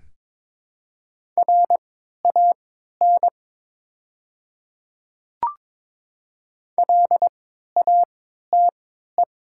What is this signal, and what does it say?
Morse code, heard as a keyed tone, about 22 wpm, spaced as for 12 wpm, 700 Hz. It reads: RAN LATE
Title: 3x Character Spacing - Top 500 Words - 22wpm